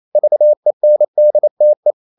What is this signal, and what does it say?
Morse code, heard as a keyed tone, about 28 wpm, 600 Hz. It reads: VENDTE
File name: Ferie QRQ 28wpm